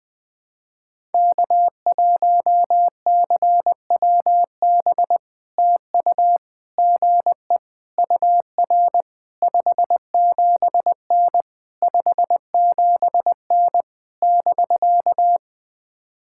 Morse code, heard as a keyed tone, about 20 wpm, 700 Hz. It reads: K1CWB TU GE UR 57N 57N <BK>
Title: POTA QSO Practice - 20wpm